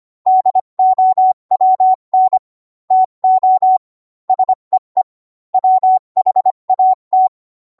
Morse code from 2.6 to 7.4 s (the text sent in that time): TO SEE WHAT